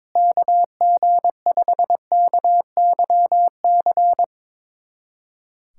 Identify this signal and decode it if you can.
Morse code, heard as a keyed tone, about 22 wpm, 700 Hz. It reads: KG5KYC
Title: USA Callsign DE Callsign - 6 Characters - 22wpm